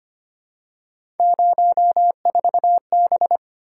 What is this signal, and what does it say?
Morse code, heard as a keyed tone, about 25 wpm, 700 Hz. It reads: 04B
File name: Three Letter-Number Combinations - 25wpm